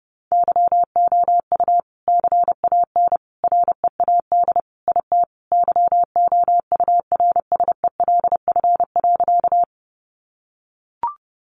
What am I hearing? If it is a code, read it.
Morse code, heard as a keyed tone, about 30 wpm, 700 Hz. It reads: YOU CAN READ IT YOURSELF.
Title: Sentences from Top 600 Words 30wpm